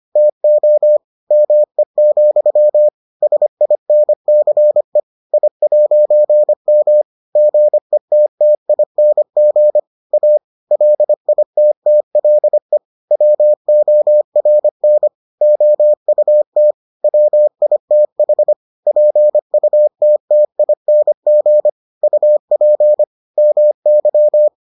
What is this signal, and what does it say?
Morse code, heard as a keyed tone, about 25 wpm, 600 Hz. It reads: TO ME, SINCE I'M GETTING A LITTLE WORN OUT WITH PUTTING UP MY